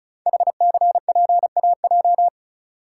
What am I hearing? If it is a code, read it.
Morse code, heard as a keyed tone, about 36 wpm, 700 Hz. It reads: HCPAJ
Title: Code Groups - 5 Letters - 35wpm